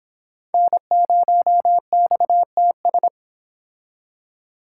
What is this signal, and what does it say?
Morse code, heard as a keyed tone, about 26 wpm, 700 Hz. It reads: N0XTS